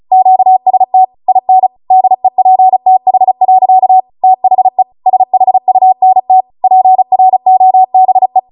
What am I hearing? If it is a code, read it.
Morse code, heard as a keyed tone, about 36 wpm, 750 Hz. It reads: QST IN DEPTH. THE SHUNT PROBE